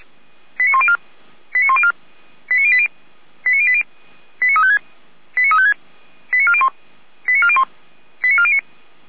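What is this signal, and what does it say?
Five-tone selective call (ZVEI), heard as quick runs of five tones, tones of about 70 ms. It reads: 89104 89104 89080 89080 89356 89356 89401 89401 89409
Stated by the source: ZVEI five-tone call 89350 received by SDR.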